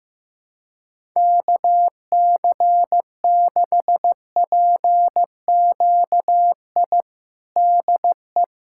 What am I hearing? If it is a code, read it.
Morse code, heard as a keyed tone, about 16 wpm, 700 Hz. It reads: KC6PQI DE
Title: USA Callsign DE Callsign - 6 Characters - 15wpm